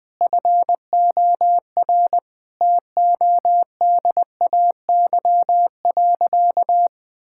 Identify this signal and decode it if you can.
Morse code, heard as a keyed tone, about 20 wpm, 700 Hz. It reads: FOR TODAY.
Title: Sentences from Top 1000 Words - Review Part 5 - 20wpm